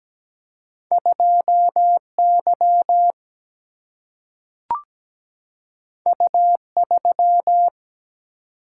Morse code, heard as a keyed tone, about 17 wpm, 700 Hz. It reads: 2Y U3
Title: Single-Letter Single-Number 17wpm